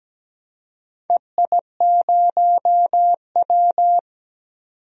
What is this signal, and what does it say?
Morse code, heard as a keyed tone, about 17 wpm, 700 Hz. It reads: EI0W
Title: DXCC-100 Call Signs - 4 Characters - 17wpm